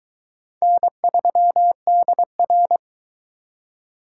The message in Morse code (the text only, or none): N3DR